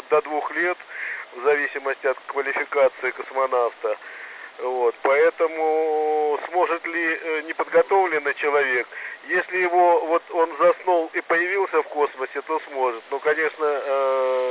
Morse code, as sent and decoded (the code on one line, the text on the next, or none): none